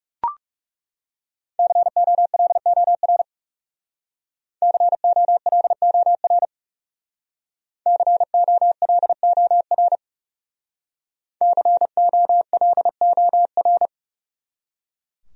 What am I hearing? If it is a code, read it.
Morse code, heard as a keyed tone, about 38 wpm, 700 Hz. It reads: COLOR COLOR COLOR COLOR